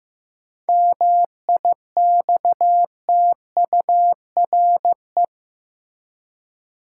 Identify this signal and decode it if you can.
Morse code, heard as a keyed tone, about 15 wpm, 700 Hz. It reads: MIXTURE
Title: Seven Letter Words - 15wpm